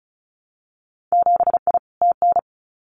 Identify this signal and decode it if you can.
Morse code, heard as a keyed tone, about 34 wpm, 700 Hz. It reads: 7I TN
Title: Winter Field Day Exchanges Encore 35wpm